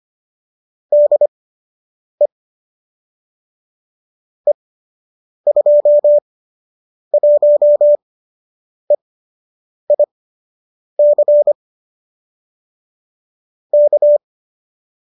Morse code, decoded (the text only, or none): DE E21EIC K